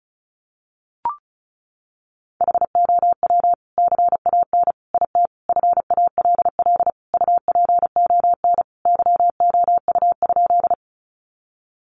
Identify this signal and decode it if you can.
Morse code, heard as a keyed tone, about 34 wpm, 700 Hz. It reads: HOW CAN IT FALL UPON YOU?
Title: Sentences from Top 1100 Words 35wpm